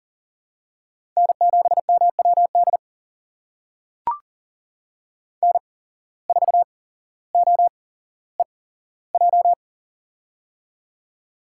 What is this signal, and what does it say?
Morse code, heard as a keyed tone, about 40 wpm, spaced as for 17 wpm, 700 Hz. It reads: N7MWD N4OEJ
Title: USA Call Signs - 8x Character Spacing - 40wpm